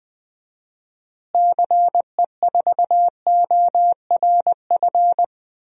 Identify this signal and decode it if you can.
Morse code, heard as a keyed tone, about 20 wpm, 700 Hz. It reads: CE4ORF
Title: DXCC-100 Call Signs - 6 Characters - Repeated 3x - 20wpm